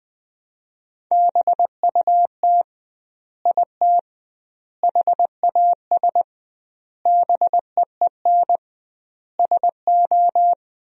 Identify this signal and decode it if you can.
Morse code, heard as a keyed tone, about 20 wpm, 700 Hz. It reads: BUT IT HAS BEEN SO
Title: 2x Word-Spacing - Sentences from Top 100 Words - 20wpm